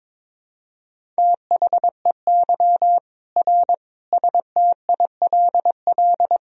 Morse code, heard as a keyed tone, about 22 wpm, 700 Hz. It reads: THEY R STILL
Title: Sentences from Top 400 Words and Abbreviations 22wpm